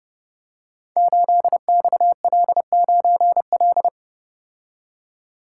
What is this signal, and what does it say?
Morse code, heard as a keyed tone, about 29 wpm, 700 Hz. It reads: 8XL9L
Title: Code Groups - 5 Letter:Numbers - 30wpm